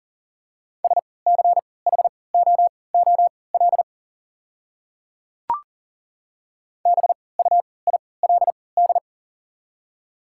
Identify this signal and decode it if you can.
Morse code, heard as a keyed tone, about 40 wpm, spaced as for 23 wpm, 700 Hz. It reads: SCHOOL BUILD
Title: 3x Character Spacing - Top 300 Words - 40wpm